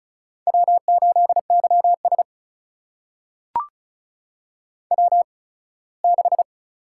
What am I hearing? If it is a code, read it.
Morse code, heard as a keyed tone, about 36 wpm, spaced as for 21 wpm, 700 Hz. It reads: W8YS W6
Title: USA Call Signs - 4 Characters - 8x Character Spacing - 35wpm